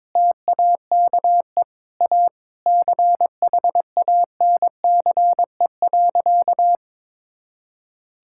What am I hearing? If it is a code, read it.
Morse code, heard as a keyed tone, about 22 wpm, 700 Hz. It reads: TAKE A CHANCE.
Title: Sentences from Top 1000 Words - Review Part 6 - 22wpm